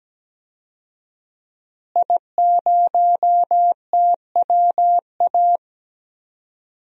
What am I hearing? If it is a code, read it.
Morse code, heard as a keyed tone, about 17 wpm, 700 Hz. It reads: I0TWA
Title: DXCC-100 Call Signs - 5 Characters - Repeated 3x - 17wpm